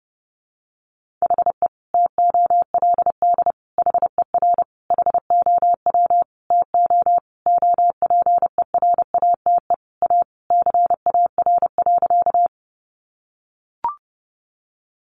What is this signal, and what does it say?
Morse code, heard as a keyed tone, about 30 wpm, 700 Hz. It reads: HE TOLD HER HOW TO OPERATE A CAR.
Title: Sentences from Top 1500 Words 30wpm